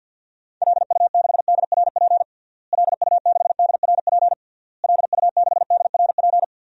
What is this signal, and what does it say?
Morse code, heard as a keyed tone, about 52 wpm, 700 Hz. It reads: LU6DRP LU6DRP LU6DRP
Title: DXCC-100 Call Signs - 6 Characters - Repeated 3x - 50wpm